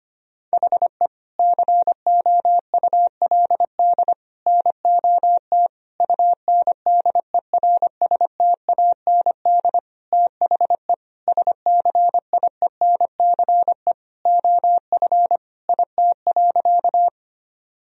Morse code, heard as a keyed tone, about 25 wpm, 700 Hz. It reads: HE COULD NOT UNDERSTAND THE SCIENCE OF IT.
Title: Sentences from Top 1000 Words - Review Part 2 - 25wpm